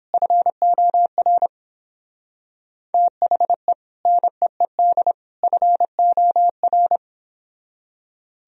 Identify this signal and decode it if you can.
Morse code, heard as a keyed tone, about 31 wpm, 700 Hz. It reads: FOR THE NEED FOR